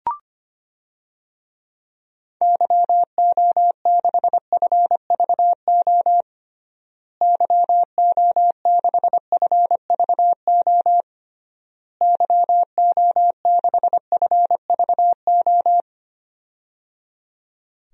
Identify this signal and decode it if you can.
Morse code, heard as a keyed tone, about 25 wpm, 700 Hz. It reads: YO6FVO YO6FVO YO6FVO
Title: DXCC-100 Call Signs - 6 Characters - Repeated 3x - 25wpm